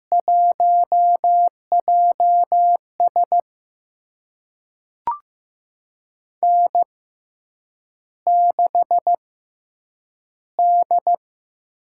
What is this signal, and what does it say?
Morse code, heard as a keyed tone, about 15 wpm, spaced as for 8 wpm, 700 Hz. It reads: 1JS N6D
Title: USA Call Signs - 5 Characters - 6x Character Spacing - 15wpm